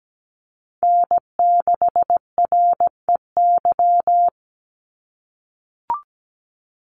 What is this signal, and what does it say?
Morse code, heard as a keyed tone, about 17 wpm, 700 Hz. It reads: N6REY
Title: USA Call Signs 17wpm